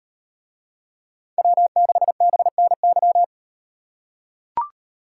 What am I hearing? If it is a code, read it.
Morse code, heard as a keyed tone, about 38 wpm, 700 Hz. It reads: W6BNY